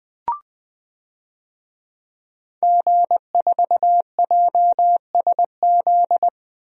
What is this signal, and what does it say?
Morse code, heard as a keyed tone, about 20 wpm, 700 Hz. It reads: G4JSZ
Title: DXCC-100 Call Signs - 5 Characters - Repeated 3x - 20wpm